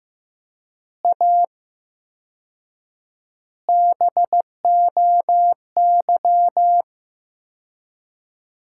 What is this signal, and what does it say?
Morse code, heard as a keyed tone, about 15 wpm, 700 Hz. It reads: A BOY
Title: 4x Word-Spacing - Sentences from Top 400 Words - 15wpm